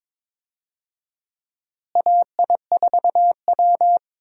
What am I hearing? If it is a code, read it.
Morse code, heard as a keyed tone, about 22 wpm, 700 Hz. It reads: AI4W
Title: USA Call Signs - 4 Characters - Repeated 2x - 22wpm